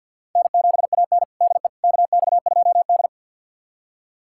Morse code, heard as a keyed tone, about 52 wpm, 700 Hz. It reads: N7AN DE KX2D
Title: USA Callsign DE Callsign - 4 Characters - 50wpm